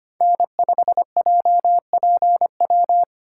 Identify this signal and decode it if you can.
Morse code, heard as a keyed tone, about 25 wpm, 700 Hz. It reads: N5JPW
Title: USA Call Signs - 5 Characters - 4x Character Spacing - 25wpm